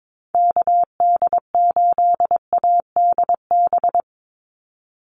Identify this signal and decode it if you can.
Morse code, heard as a keyed tone, about 22 wpm, 700 Hz. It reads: KD8ADB
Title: USA Call Signs 22wpm